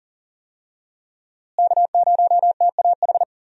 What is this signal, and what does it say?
Morse code, heard as a keyed tone, about 40 wpm, 700 Hz. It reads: K0TAH
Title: USA Call Signs - 5 Characters - Repeated 3x - 40wpm